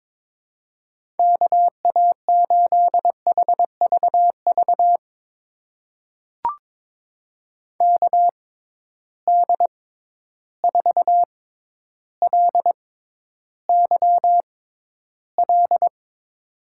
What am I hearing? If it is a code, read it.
Morse code, heard as a keyed tone, about 22 wpm, spaced as for 11 wpm, 700 Hz. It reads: KA8HVV KD4LYL
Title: USA Call Signs - 6 Characters - 6x Character Spacing - 22wpm